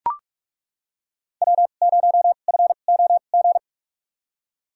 Morse code, heard as a keyed tone, about 42 wpm, spaced as for 31 wpm, 700 Hz. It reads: W0FOG
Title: USA Call Signs - 5 Characters - 2x Character Spacing - 45wpm